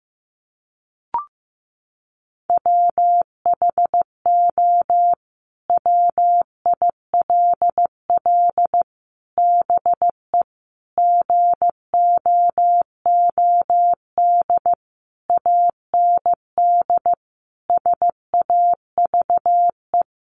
Morse code, heard as a keyed tone, about 15 wpm, 700 Hz. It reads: WHO WILL BE GOOD AND SAVE
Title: Sentences from Top 1100 Words 15wpm